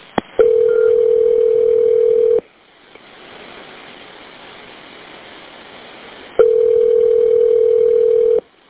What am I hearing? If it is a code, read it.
Ringback tone (North America), 440 + 480 Hz.